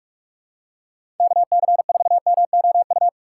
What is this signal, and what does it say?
Morse code, heard as a keyed tone, about 46 wpm, 700 Hz. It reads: KC4MOU
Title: USA Call Signs - 6 Characters - 6x Character Spacing - 45wpm